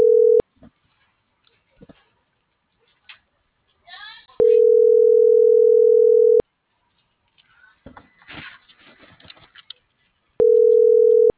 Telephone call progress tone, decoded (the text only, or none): ringback tone